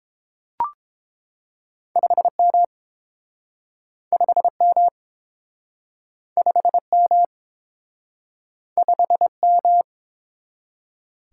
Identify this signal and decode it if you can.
Morse code, heard as a keyed tone, about 30 wpm, 700 Hz. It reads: 5M 5M 5M 5M